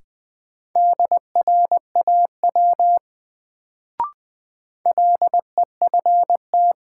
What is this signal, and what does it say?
Morse code, heard as a keyed tone, about 20 wpm, 700 Hz. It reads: DRAW LEFT
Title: Sentences from Top 300 Words - New Word Drill - 20wpm